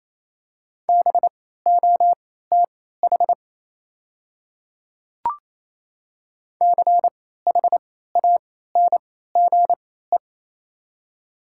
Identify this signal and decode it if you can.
Morse code, heard as a keyed tone, about 28 wpm, spaced as for 16 wpm, 700 Hz. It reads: BOTH CHANGE
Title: 3x Character Spacing - Top 300 Words - 28wpm